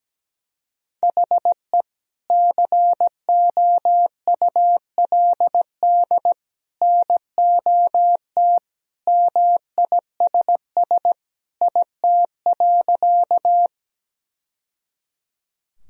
Morse code, heard as a keyed tone, about 17 wpm, 700 Hz. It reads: HE COULD NOT MISS IT.
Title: Sentences from Top 1000 Words - Review Part 3 - 17wpm